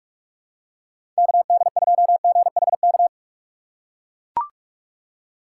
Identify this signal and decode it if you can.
Morse code, heard as a keyed tone, about 43 wpm, 700 Hz. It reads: KD2GHK
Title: USA Call Signs - 6 Characters - 6x Character Spacing - 45wpm